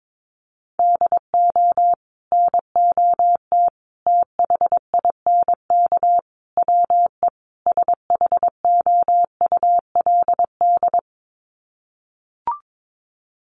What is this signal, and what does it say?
Morse code, heard as a keyed tone, about 22 wpm, 700 Hz. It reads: DO NOT THINK WE SHOULD
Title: Sets of 5 Words Encore 22wpm